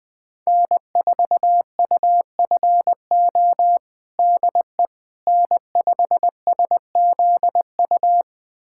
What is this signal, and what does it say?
Morse code, heard as a keyed tone, about 20 wpm, 700 Hz. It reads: N4UFO DE N5SZU
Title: USA Callsign DE Callsign - 5 Characters - 20wpm